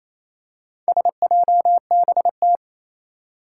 Morse code, heard as a keyed tone, about 27 wpm, 700 Hz. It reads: SJBT
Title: Four Challenging Letter-Number Combinations 28wpm